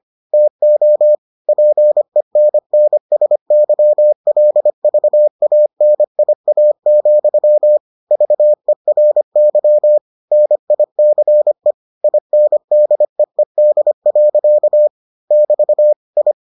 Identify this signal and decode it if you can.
Morse code, heard as a keyed tone, about 25 wpm, 600 Hz. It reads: TO PENNSYLVANIA, VERY NICE INDEED. <BT> I